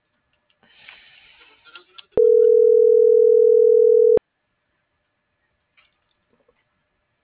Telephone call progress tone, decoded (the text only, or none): ringback tone